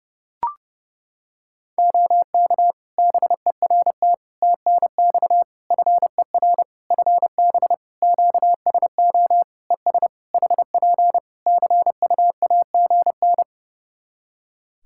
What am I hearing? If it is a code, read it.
Morse code, heard as a keyed tone, about 30 wpm, 700 Hz. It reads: OK BERT TNX FER FB QSO ES HP CUAGN
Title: Practice QSO - Exchange 4 - Ending - 30wpm